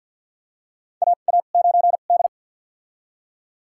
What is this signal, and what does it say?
Morse code, heard as a keyed tone, about 46 wpm, spaced as for 35 wpm, 700 Hz. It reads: AA9D
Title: USA Call Signs - 4 Characters - 2x Character Spacing - 50wpm